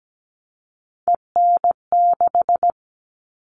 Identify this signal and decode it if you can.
Morse code, heard as a keyed tone, about 17 wpm, 700 Hz. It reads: EN6
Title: All Cut Numbers - 3-Digit 17wpm